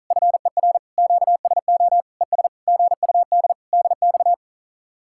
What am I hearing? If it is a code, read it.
Morse code, heard as a keyed tone, about 46 wpm, 700 Hz. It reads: FER QSO ES GUD DX